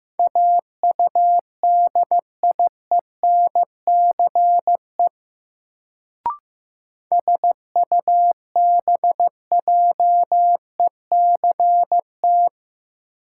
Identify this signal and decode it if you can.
Morse code, heard as a keyed tone, about 15 wpm, 700 Hz. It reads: AUDIENCE SUBJECT
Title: Sentences from Top 800 Words - New Word Drill - 15wpm